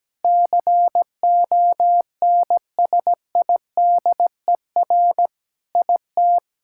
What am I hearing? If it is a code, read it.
Morse code, heard as a keyed tone, about 17 wpm, 700 Hz. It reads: CONSIDER IT
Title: Sentences from Top 1000 Words - Review Part 5 - 17wpm